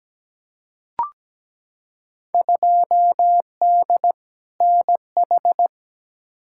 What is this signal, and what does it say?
Morse code, heard as a keyed tone, about 17 wpm, 700 Hz. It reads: 2D NH
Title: Field Day Exchanges Encore - 17wpm